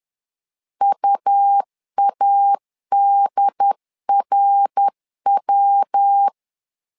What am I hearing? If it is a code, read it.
Morse code, heard as a keyed tone, about 10 wpm, 800 Hz. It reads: UADRW
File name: LEZIONI / 3 MWRO / 3 MWR 10wpm 1952